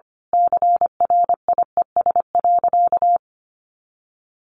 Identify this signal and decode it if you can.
Morse code, heard as a keyed tone, about 25 wpm, 700 Hz. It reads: CRIES.
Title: A Sound Of Thunder 25wpm